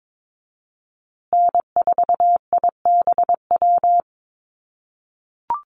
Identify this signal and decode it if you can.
Morse code, heard as a keyed tone, about 22 wpm, 700 Hz. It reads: N4IBW
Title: USA Call Signs Repeated 3x 22wpm